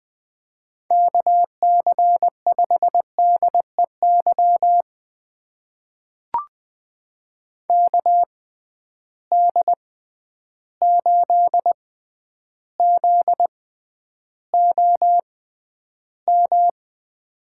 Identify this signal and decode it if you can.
Morse code, heard as a keyed tone, about 20 wpm, spaced as for 10 wpm, 700 Hz. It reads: KC5DEY KD8ZOM
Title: USA Call Signs - 6 Characters - 6x Character Spacing - 20wpm